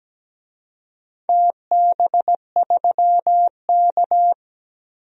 Morse code, heard as a keyed tone, about 17 wpm, 700 Hz. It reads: TB3K
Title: DXCC-100 Call Signs - 4 Characters - Repeated 2x - 17wpm